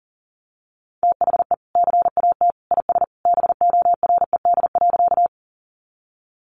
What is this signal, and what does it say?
Morse code, heard as a keyed tone, about 40 wpm, 700 Hz. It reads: THE CAT IS BORED.
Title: Easy Phrases 40wpm